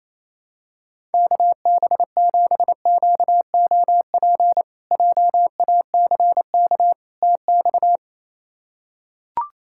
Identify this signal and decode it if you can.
Morse code, heard as a keyed tone, about 28 wpm, 700 Hz. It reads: KB7QOP JACK TX
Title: K1USN SST QSO Practice - 28wpm